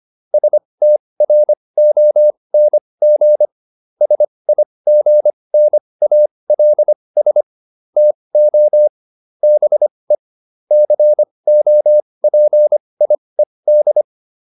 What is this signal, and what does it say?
Morse code, heard as a keyed tone, about 25 wpm, spaced as for 20 wpm, 600 Hz. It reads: STRONG SIGNALS TO BE COPIED